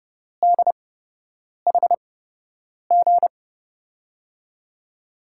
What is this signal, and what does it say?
Morse code, heard as a keyed tone, about 30 wpm, spaced as for 8 wpm, 700 Hz. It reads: DHG
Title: USA Call Signs - 8x Character Spacing - 30wpm